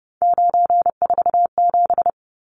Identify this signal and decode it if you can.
Morse code, heard as a keyed tone, about 30 wpm, 700 Hz. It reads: 947
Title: Numbers - 3-Digit 30wpm